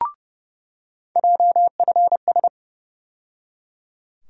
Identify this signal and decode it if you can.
Morse code, heard as a keyed tone, about 30 wpm, 700 Hz. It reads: JFS
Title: USA Call Sign Postfixes - 3 Letters 30wpm